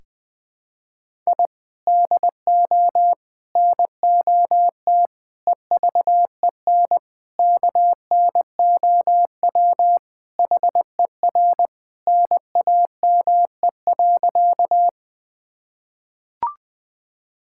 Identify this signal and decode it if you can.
Morse code, heard as a keyed tone, about 20 wpm, 700 Hz. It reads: I DO NOT EVEN KNOW HER NAME.
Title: Sentences with 5-Word Phrases - 20wpm